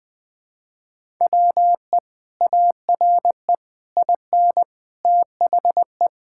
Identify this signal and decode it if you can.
Morse code, heard as a keyed tone, about 20 wpm, 700 Hz. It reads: WE ARE IN THE
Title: Sentences from Top 1000 Words - Review Part 6 - 20wpm